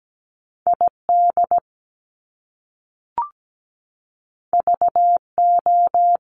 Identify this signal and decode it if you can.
Morse code, heard as a keyed tone, about 17 wpm, 700 Hz. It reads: ID VO
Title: Top Two Letter Combinations in Words 17wpm